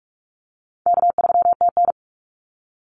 Morse code, heard as a keyed tone, about 46 wpm, 700 Hz. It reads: K3TN
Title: Easy Call Signs - 45wpm